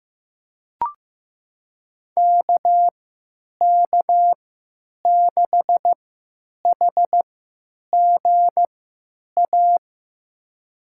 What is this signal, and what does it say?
Morse code, heard as a keyed tone, about 15 wpm, spaced as for 9 wpm, 700 Hz. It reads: KK6HGA
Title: USA Call Signs - 3x Character Spacing - 15wpm